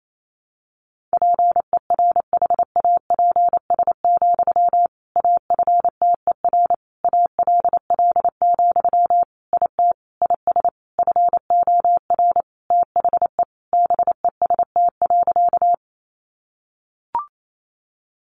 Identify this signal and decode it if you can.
Morse code, heard as a keyed tone, about 28 wpm, 700 Hz. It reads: PERHAPS, AFTER ALL, IT IS FOR THE BEST.
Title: Sentences from Top 500 Words 28wpm